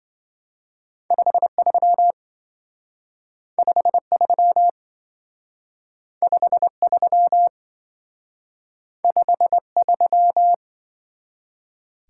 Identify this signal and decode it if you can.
Morse code, heard as a keyed tone, about 25 wpm, 700 Hz. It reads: 53 53 53 53